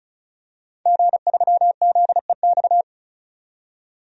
Morse code, heard as a keyed tone, about 36 wpm, 700 Hz. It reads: G3ZEX
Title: DXCC-100 Call Signs - 5 Characters - 35wpm